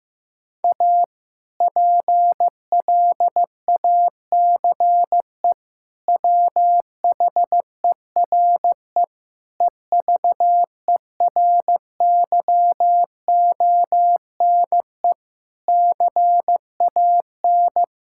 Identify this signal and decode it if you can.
Morse code, heard as a keyed tone, about 15 wpm, 700 Hz. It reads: A PLACE WHERE EVERYONE CAN
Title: Sentences from Top 1000 Words - Review Part 5 - 15wpm